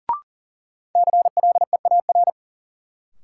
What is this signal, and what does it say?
Morse code, heard as a keyed tone, about 40 wpm, 700 Hz. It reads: CLEAR
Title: Sets of 5 Words Encore - New Word Drill - 40wpm